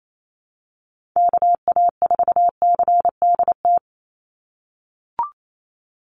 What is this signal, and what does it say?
Morse code, heard as a keyed tone, about 28 wpm, 700 Hz. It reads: KA4CDT
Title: USA Call Signs 28wpm